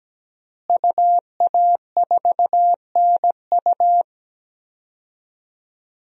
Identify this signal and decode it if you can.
Morse code, heard as a keyed tone, about 17 wpm, 700 Hz. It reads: UA4NU
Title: DXCC-100 Call Signs - 5 Characters - Repeated 3x - 17wpm